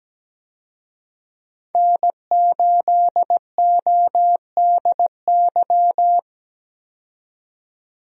Morse code, heard as a keyed tone, about 17 wpm, 700 Hz. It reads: N8ODY
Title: USA Call Signs - 5 Characters - 17wpm